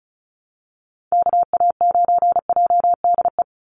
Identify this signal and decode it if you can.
Morse code, heard as a keyed tone, about 34 wpm, 700 Hz. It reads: KA9JDE
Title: USA Call Signs 35wpm